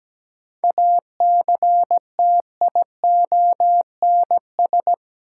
Morse code, heard as a keyed tone, about 17 wpm, 700 Hz. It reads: ACTIONS
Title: Sentences from Top 1000 Words - Review Part 6 - New Word Drill - 17wpm